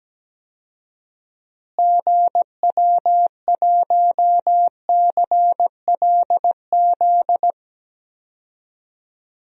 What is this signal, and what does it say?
Morse code, heard as a keyed tone, about 17 wpm, 700 Hz. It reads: GW1CLZ
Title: DXCC-100 Call Signs - 6 Characters - Repeated 3x - 17wpm